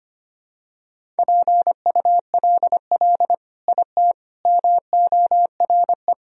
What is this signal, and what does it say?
Morse code, heard as a keyed tone, about 25 wpm, 700 Hz. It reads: PULL IT MORE
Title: Sentences from Top 400 Words and Abbreviations 25wpm